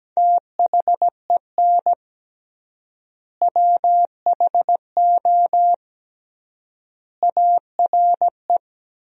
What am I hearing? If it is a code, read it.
Morse code, heard as a keyed tone, about 17 wpm, 700 Hz. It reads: THEN WHO ARE
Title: 3x Word-Spacing - Sentences from Top 100 Words - 17wpm